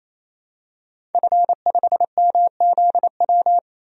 Morse code, heard as a keyed tone, about 28 wpm, 700 Hz. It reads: F5MZW
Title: DXCC-100 Call Signs - 5 Characters - 28wpm